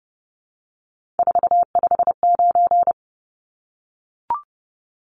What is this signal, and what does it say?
Morse code, heard as a keyed tone, about 30 wpm, 700 Hz. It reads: 459
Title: Numbers - 3-Digit 30wpm